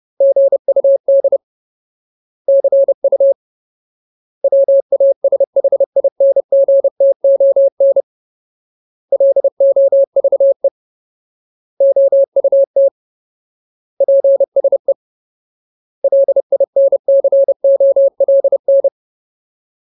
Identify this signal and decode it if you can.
Morse code, heard as a keyed tone, about 30 wpm, 550 Hz. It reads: GUD CU WASHINGTON LOVE OUT PSE LINCOLN
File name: Ultimate Word List / 30WPM Ultimate